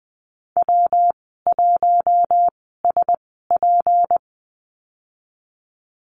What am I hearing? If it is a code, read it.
Morse code, heard as a keyed tone, about 20 wpm, spaced as for 15 wpm, 700 Hz. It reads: W1SP
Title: 154 Lesson 25g - Letter p - Call Signs 20wpm